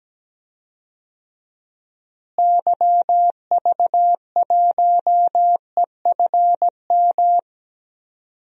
Morse code, heard as a keyed tone, about 17 wpm, 700 Hz. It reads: YV1EFM
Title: DXCC-100 Call Signs - 6 Characters - Repeated 3x - 17wpm